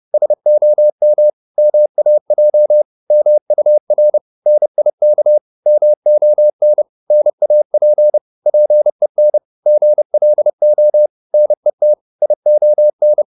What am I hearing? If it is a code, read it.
Morse code, heard as a keyed tone, about 30 wpm, 600 Hz. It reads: SOM MAJ MUR NIK MON NAP PEN GLO NET ION
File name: Korte ord DK 30wpm